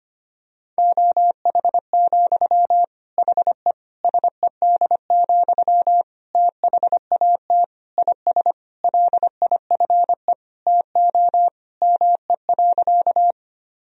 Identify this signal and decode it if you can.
Morse code, heard as a keyed tone, about 25 wpm, 700 Hz. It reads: OH, HE SED, THAT IS LIFE TO ME.
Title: Sentences from Top 500 Words and Abbreviations 25wpm